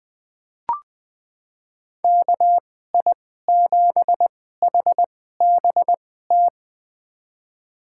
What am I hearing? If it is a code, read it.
Morse code, heard as a keyed tone, about 20 wpm, spaced as for 14 wpm, 700 Hz. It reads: KI7HBT
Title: USA Call Signs - 2x Character Spacing - 20wpm